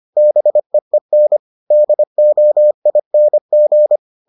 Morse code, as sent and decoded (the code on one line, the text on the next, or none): -... . -. / -.. --- .. -. --.
BEEN DOING